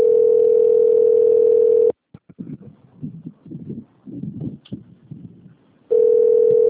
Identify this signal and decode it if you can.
Ringback tone (North America), 440 + 480 Hz.
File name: out- - - - - .67940